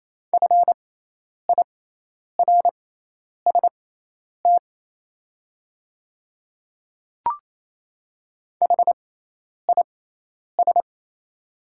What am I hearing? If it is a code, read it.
Morse code, heard as a keyed tone, about 28 wpm, spaced as for 10 wpm, 700 Hz. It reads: FIRST HIS
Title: 6x Character Spacing - Top 100 Words - 28wpm